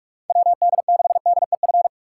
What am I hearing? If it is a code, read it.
Morse code, heard as a keyed tone, about 41 wpm, 700 Hz. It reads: WD6DEF
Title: USA Call Signs - 6 Characters - 3x Character Spacing - 45wpm